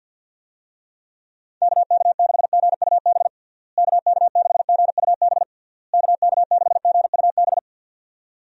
Morse code, heard as a keyed tone, about 51 wpm, 700 Hz. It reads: KK6GUB KK6GUB KK6GUB
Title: USA Call Signs - 6 Characters - Repeated 3x - 50wpm